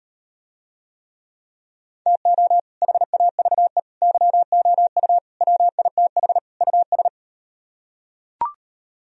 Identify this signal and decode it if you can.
Morse code, heard as a keyed tone, about 38 wpm, 700 Hz. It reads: TO HAVE YOU WITH US